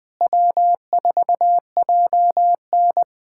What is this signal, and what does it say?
Morse code, heard as a keyed tone, about 20 wpm, 700 Hz. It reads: W4JN
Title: USA Callsign DE Callsign - 4 Characters - 20wpm